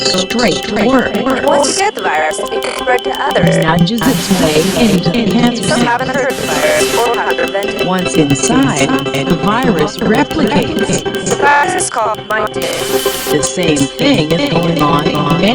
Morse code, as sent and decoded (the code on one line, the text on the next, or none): none